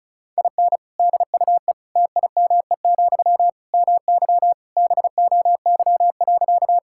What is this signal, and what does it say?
Morse code, heard as a keyed tone, about 34 wpm, 700 Hz. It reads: IN DUE TIME, MY BOY.
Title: Sentences from Top 1000 Words - Review Part 6 - 35wpm